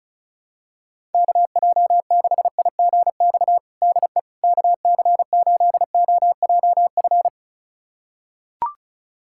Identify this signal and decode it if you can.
Morse code, heard as a keyed tone, about 34 wpm, 700 Hz. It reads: KJ6IGX DE KC8OJF
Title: USA Callsign DE Callsign - 6 Characters - 35wpm